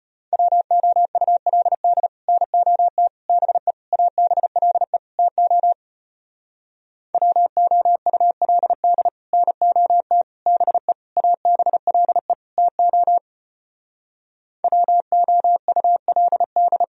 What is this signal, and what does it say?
Morse code, heard as a keyed tone, about 33 wpm, 700 Hz. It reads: WOULD NOT BE ABLE TO WOULD NOT BE ABLE TO WOULD